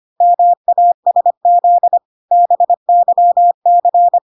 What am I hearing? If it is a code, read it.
Morse code, heard as a keyed tone, about 25 wpm, 700 Hz. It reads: MASZ BYC